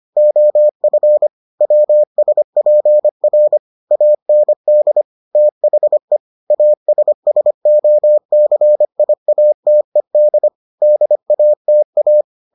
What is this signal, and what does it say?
Morse code, heard as a keyed tone, about 25 wpm, 600 Hz. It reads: OF WSPR AND THE ASSOCIATED DATA